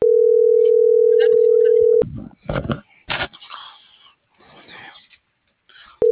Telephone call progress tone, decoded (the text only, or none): ringback tone